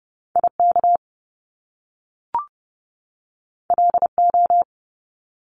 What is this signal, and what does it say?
Morse code, heard as a keyed tone, about 30 wpm, 700 Hz. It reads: IK LO
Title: Top Two Letter Combinations in Words 30wpm